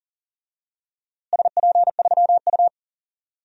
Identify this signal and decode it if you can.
Morse code, heard as a keyed tone, about 40 wpm, 700 Hz. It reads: SP3U
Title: DXCC-100 Call Signs - 4 Characters - Repeated 3x - 40wpm